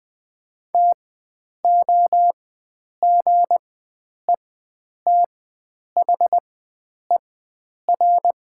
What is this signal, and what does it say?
Morse code, heard as a keyed tone, about 20 wpm, spaced as for 9 wpm, 700 Hz. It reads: TOGETHER
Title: 4x Character Spacing - Top 300 Words - 20wpm